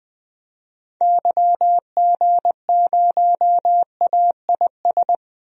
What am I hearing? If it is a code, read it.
Morse code, heard as a keyed tone, about 21 wpm, 700 Hz. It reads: YG0AIS
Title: DXCC-100 Call Signs - 6 Characters - Repeated 3x - 20wpm